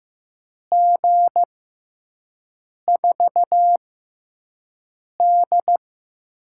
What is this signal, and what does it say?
Morse code, heard as a keyed tone, about 15 wpm, spaced as for 5 wpm, 700 Hz. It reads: G4D
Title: USA Call Signs - 6 Characters - 6x Character Spacing - 15wpm